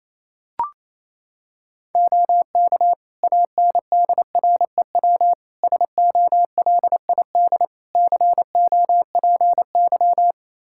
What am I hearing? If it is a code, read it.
Morse code, heard as a keyed tone, about 28 wpm, 700 Hz. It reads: OK ANDREW SOLID COPY
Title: Practice QSO - Exchange 3 - More Info - 28wpm